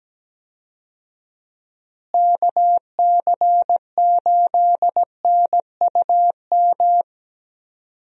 Morse code, heard as a keyed tone, about 17 wpm, 700 Hz. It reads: KC8NUM
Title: USA Call Signs - 6 Characters - Repeated 3x - 17wpm